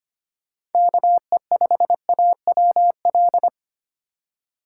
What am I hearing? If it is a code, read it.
Morse code, heard as a keyed tone, about 25 wpm, 700 Hz. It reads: KE5AWL
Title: USA Call Signs - 6 Characters - 3x Character Spacing - 25wpm